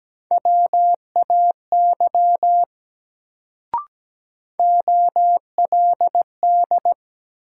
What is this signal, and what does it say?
Morse code, heard as a keyed tone, about 17 wpm, 700 Hz. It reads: WAY OLD
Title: Sets of 2 Words - New Word Drill - 17wpm